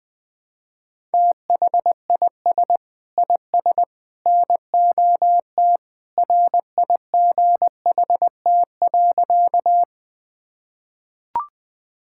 Morse code, heard as a keyed tone, about 21 wpm, 700 Hz. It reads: THIS IS NOT RIGHT.
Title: Sentences with 3-Word Phrases - 20wpm